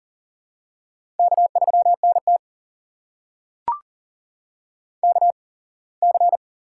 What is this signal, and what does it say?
Morse code, heard as a keyed tone, about 38 wpm, spaced as for 24 wpm, 700 Hz. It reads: K3NT KC
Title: USA Call Signs - 8x Character Spacing - 40wpm